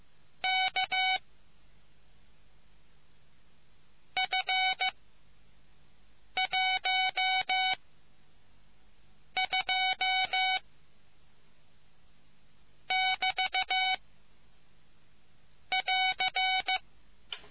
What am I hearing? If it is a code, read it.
Morse code, heard as a keyed tone, about 15 wpm, spaced as for 4 wpm, 750 Hz. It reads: KF12 <BT> <AR>